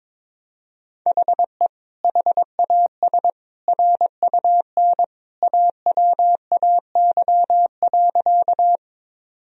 Morse code, heard as a keyed tone, about 22 wpm, 700 Hz. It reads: HE HAS RUN AWAY.
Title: Sentences from Top 1000 Words - Review Part 1 - 22wpm